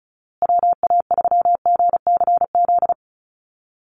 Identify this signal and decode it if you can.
Morse code, heard as a keyed tone, about 34 wpm, 700 Hz. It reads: WA3GCZ
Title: USA Call Signs Repeated 3x 35wpm